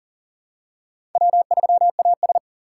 Morse code, heard as a keyed tone, about 38 wpm, 700 Hz. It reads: W3AS
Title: USA Call Signs - 4 Characters - 6x Character Spacing - 40wpm